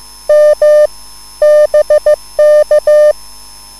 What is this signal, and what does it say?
Morse code, heard as a keyed tone, about 15 wpm, 600 Hz. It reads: M BK
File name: QSL / CW lesson